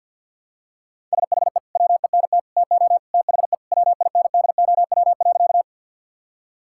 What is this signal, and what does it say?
Morse code, heard as a keyed tone, about 46 wpm, 700 Hz. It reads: SHE WENT TO THE WINDOW.
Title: Sentences from Top 1600 Words 50wpm